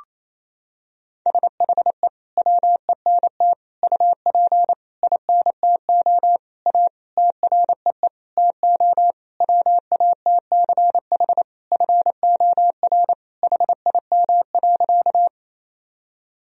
Morse code, heard as a keyed tone, about 28 wpm, 700 Hz. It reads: SHE WENT UP INTO A TREE TO WATCH FOR HIM.
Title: Sentences from Top 1000 Words - Review Part 1 - 28wpm